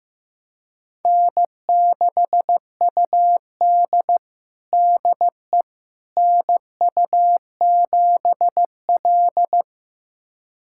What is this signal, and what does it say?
Morse code, heard as a keyed tone, about 15 wpm, 700 Hz. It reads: N6UD DE NU7L
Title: USA Callsign DE Callsign - 4 Characters - 15wpm